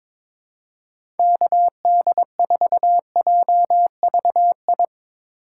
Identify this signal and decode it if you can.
Morse code, heard as a keyed tone, about 22 wpm, 700 Hz. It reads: KD4JVI
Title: USA Call Signs - 6 Characters - 2x Character Spacing - 22wpm